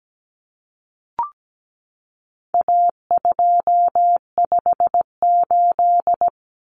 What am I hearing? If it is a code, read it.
Morse code, heard as a keyed tone, about 17 wpm, 700 Hz. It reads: A258
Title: Common Cut Numbers - 4-Digit 17wpm